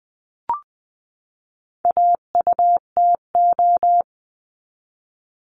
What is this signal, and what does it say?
Morse code, heard as a keyed tone, about 20 wpm, 700 Hz. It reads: AUTO
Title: 259 Lesson 42d - Words 18-20wpm